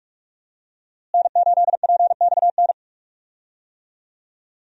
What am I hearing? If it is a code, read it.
Morse code, heard as a keyed tone, about 46 wpm, 700 Hz. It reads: N8PXN
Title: USA Call Signs - 5 Characters - Repeated 3x - 45wpm